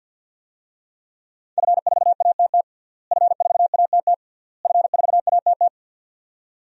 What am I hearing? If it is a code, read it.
Morse code, heard as a keyed tone, about 52 wpm, 700 Hz. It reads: F4ATT F4ATT F4ATT
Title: DXCC-100 Call Signs - 5 Characters - Repeated 3x - 50wpm